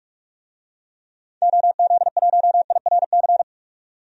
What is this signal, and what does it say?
Morse code, heard as a keyed tone, about 43 wpm, 700 Hz. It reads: OZ1IRC
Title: DXCC-100 Call Signs - 6 Characters - 45wpm